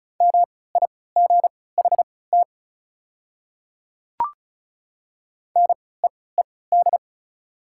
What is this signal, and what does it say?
Morse code, heard as a keyed tone, about 34 wpm, spaced as for 20 wpm, 700 Hz. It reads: MIGHT NEED
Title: 3x Character Spacing - Top 500 Words - 35wpm